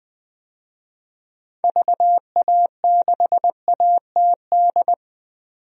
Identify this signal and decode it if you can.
Morse code, heard as a keyed tone, about 20 wpm, 700 Hz. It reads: VA6ATD
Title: DXCC-100 Call Signs - 6 Characters - 20wpm